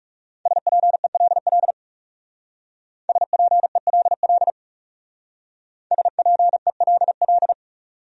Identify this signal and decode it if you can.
Morse code, heard as a keyed tone, about 36 wpm, 700 Hz. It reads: SPELL SPELL SPELL
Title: Speed-Racing Top 200 Words 30wpm